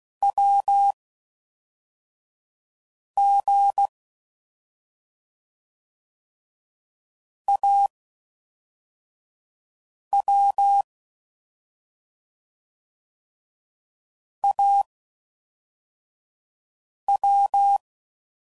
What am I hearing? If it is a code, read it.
Morse code, heard as a keyed tone, about 16 wpm, 800 Hz. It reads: W G A W A W